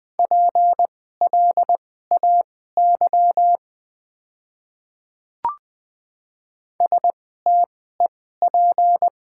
Morse code, heard as a keyed tone, about 20 wpm, spaced as for 14 wpm, 700 Hz. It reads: PLAY STEP
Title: 2x Character Spacing - Top 500 Words - 20wpm